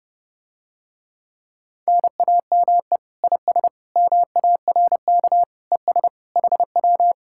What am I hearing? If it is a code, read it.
Morse code, heard as a keyed tone, about 31 wpm, 700 Hz. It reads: NAME IS MARK ES HW